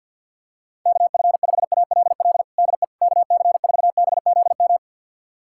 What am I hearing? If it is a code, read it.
Morse code, heard as a keyed tone, about 51 wpm, 700 Hz. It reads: KF5ALL DE KC4BZM
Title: USA Callsign DE Callsign - 6 Characters - 50wpm